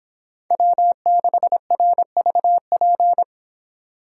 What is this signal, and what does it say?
Morse code, heard as a keyed tone, about 25 wpm, 700 Hz. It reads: W6RVP